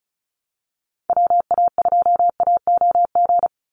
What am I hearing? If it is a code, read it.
Morse code, heard as a keyed tone, about 34 wpm, 700 Hz. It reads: WA2AOG
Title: Call Signs Repeated 2x - 35wpm